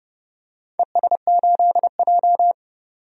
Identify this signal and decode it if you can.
Morse code, heard as a keyed tone, about 30 wpm, 700 Hz. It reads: ES8J
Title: DXCC-100 Call Signs - 4 Characters - 30wpm